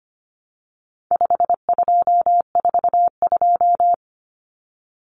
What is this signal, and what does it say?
Morse code, heard as a keyed tone, about 25 wpm, 700 Hz. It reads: 5242
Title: Numbers - 4-Digit 25wpm